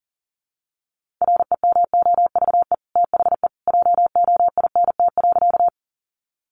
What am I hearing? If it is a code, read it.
Morse code, heard as a keyed tone, about 40 wpm, 700 Hz. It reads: REMOVE THE JOINT.
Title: Sentences from Top 1300 Words 40wpm